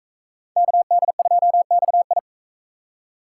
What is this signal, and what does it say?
Morse code, heard as a keyed tone, about 41 wpm, 700 Hz. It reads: KD2XI